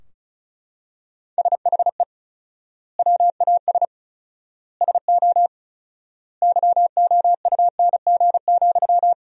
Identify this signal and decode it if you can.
Morse code, heard as a keyed tone, about 36 wpm, 700 Hz. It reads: SHE WAS SO YOUNG,
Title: 4x Word-Spacing - Sentences from Top 400 Words - 35wpm